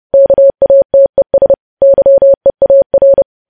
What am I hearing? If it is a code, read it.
Morse code, heard as a keyed tone, about 30 wpm, 550 Hz. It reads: KATES YEAR
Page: news headlines in morse code at 30 wpm